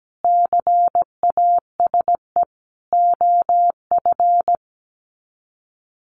Morse code, heard as a keyed tone, about 17 wpm, 700 Hz. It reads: CASE OF
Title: Sets of 5 Words 17wpm